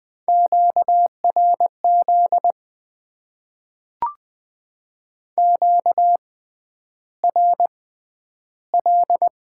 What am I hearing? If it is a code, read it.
Morse code, heard as a keyed tone, about 20 wpm, spaced as for 10 wpm, 700 Hz. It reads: QRZ QRL
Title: Top 90 QSO Elements - 6x Character Spacing - 20wpm